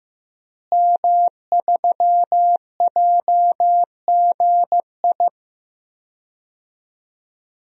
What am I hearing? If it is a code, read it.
Morse code, heard as a keyed tone, about 15 wpm, 700 Hz. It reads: M3JGI
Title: DXCC-100 Call Signs - 5 Characters - Repeated 2x - 15wpm